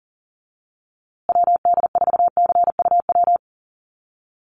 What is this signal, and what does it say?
Morse code, heard as a keyed tone, about 40 wpm, 700 Hz. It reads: WD4CUW
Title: USA Call Signs Repeated 3x 40wpm